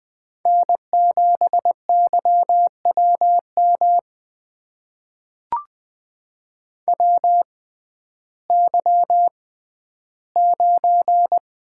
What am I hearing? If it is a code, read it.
Morse code, heard as a keyed tone, about 21 wpm, spaced as for 12 wpm, 700 Hz. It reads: N7YWM WY9
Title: USA Call Signs - 6x Character Spacing - 20wpm